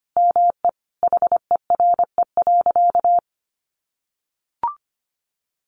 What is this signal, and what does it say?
Morse code, heard as a keyed tone, about 25 wpm, 700 Hz. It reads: ME HERE.
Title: Sentences from Top 200 Words 25wpm